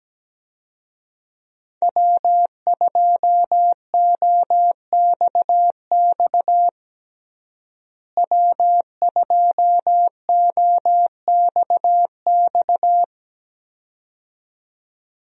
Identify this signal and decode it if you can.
Morse code, heard as a keyed tone, about 17 wpm, 700 Hz. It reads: W2OXX W2OXX
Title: USA Call Signs - 5 Characters - Repeated 2x - 17wpm